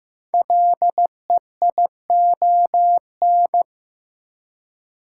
Morse code, heard as a keyed tone, about 15 wpm, 700 Hz. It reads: LEION
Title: Code Groups - 5 Letter:Numbers - 15wpm